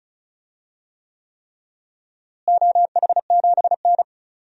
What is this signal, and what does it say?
Morse code, heard as a keyed tone, about 34 wpm, 700 Hz. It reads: OH7N
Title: DXCC-100 Call Signs - 4 Characters - Repeated 3x - 35wpm